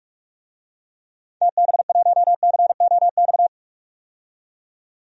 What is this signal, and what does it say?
Morse code, heard as a keyed tone, about 47 wpm, 700 Hz. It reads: TB1COX
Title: DXCC-100 Call Signs - 6 Characters - 45wpm